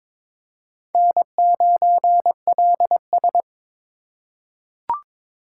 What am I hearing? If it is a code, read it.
Morse code, heard as a keyed tone, about 22 wpm, 700 Hz. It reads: N9LS
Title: USA Call Signs - 4 Characters - 2x Character Spacing - 22wpm